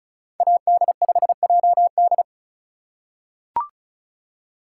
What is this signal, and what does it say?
Morse code, heard as a keyed tone, about 34 wpm, 700 Hz. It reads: AD5JD
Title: USA Call Signs - 5 Characters - 3x Character Spacing - 35wpm